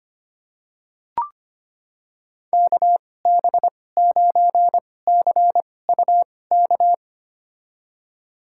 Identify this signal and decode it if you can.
Morse code, heard as a keyed tone, about 25 wpm, spaced as for 18 wpm, 700 Hz. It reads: KB9CUK
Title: USA Call Signs - 6 Characters - 2x Character Spacing - 25wpm